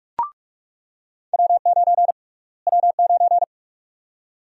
Morse code, heard as a keyed tone, about 46 wpm, 700 Hz. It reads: W9 W9
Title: USA Call Sign Prefixes - Repeated 2x - 45wpm